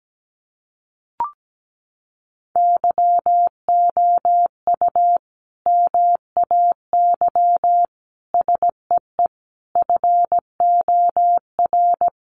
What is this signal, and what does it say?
Morse code, heard as a keyed tone, about 17 wpm, 700 Hz. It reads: YOU MAY SEE FOR
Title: Sentences from Top 600 Words 17wpm